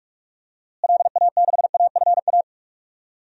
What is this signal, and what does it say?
Morse code, heard as a keyed tone, about 46 wpm, 700 Hz. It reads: LA6AFA
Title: DXCC-100 Call Signs - 6 Characters - Repeated 2x - 45wpm